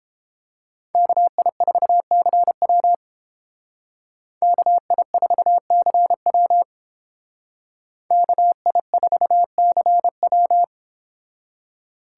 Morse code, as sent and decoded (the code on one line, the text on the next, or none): -.- .. ....- -.-. .-- / -.- .. ....- -.-. .-- / -.- .. ....- -.-. .--
KI4CW KI4CW KI4CW